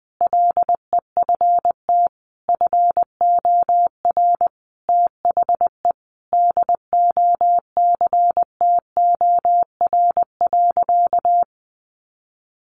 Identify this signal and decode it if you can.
Morse code, heard as a keyed tone, about 20 wpm, 700 Hz. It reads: LEFT FOR THE DOCTOR.
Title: Sentences from Top 1500 Words 20wpm